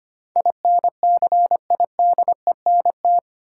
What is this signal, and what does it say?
Morse code, heard as a keyed tone, about 25 wpm, 700 Hz. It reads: INCIDENT
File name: Eight Letter Words - 25wpm